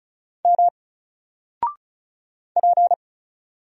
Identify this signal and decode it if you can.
Morse code, heard as a keyed tone, about 34 wpm, 700 Hz. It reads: M P
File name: Single-Letter-Number-35wpm